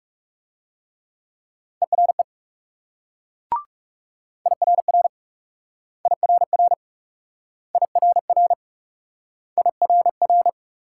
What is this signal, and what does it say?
Morse code, heard as a keyed tone, about 42 wpm, 700 Hz. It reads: ERE IRR IRR IRR IRR